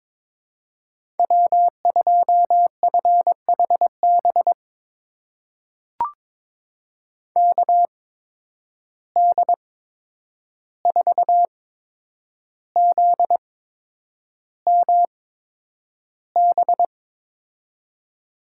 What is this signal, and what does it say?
Morse code, heard as a keyed tone, about 22 wpm, spaced as for 9 wpm, 700 Hz. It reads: W2FHB KD4ZMB
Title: USA Call Signs - 8x Character Spacing - 22wpm